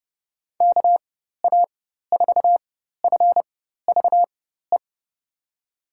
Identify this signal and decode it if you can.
Morse code, heard as a keyed tone, about 30 wpm, spaced as for 14 wpm, 700 Hz. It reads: KA4FVE
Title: USA Call Signs - 6 Characters - 4x Character Spacing - 30wpm